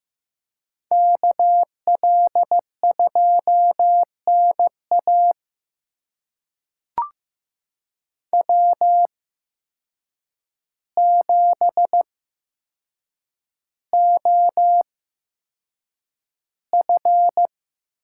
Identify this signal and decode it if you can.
Morse code, heard as a keyed tone, about 15 wpm, spaced as for 7 wpm, 700 Hz. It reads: KL2NA W7OF
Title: USA Call Signs - 8x Character Spacing - 15wpm